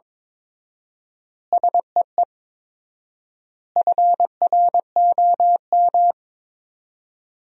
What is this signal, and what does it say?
Morse code, heard as a keyed tone, about 22 wpm, 700 Hz. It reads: SEE FROM
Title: 4x Word-Spacing - Sentences from Top 100 Words - 22wpm